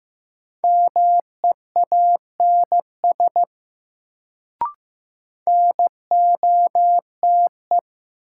Morse code, heard as a keyed tone, about 15 wpm, 700 Hz. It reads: MEANS NOTE
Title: Sets of 5 Words - New Word Drill - 15wpm